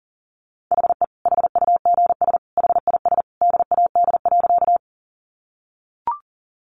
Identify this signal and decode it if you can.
Morse code, heard as a keyed tone, about 40 wpm, 700 Hz. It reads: HE HUGS HIS DAD.
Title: Easy Phrases 40wpm